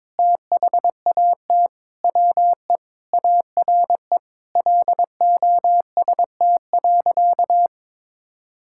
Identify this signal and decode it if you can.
Morse code, heard as a keyed tone, about 22 wpm, 700 Hz. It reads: THAT WE ARE LOST.
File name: Sentences with 7-Word Phrases - 22wpm